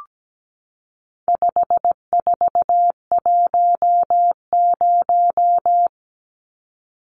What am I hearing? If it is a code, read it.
Morse code, heard as a keyed tone, about 17 wpm, 700 Hz. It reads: 5410
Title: Numbers - 4-Digit 17wpm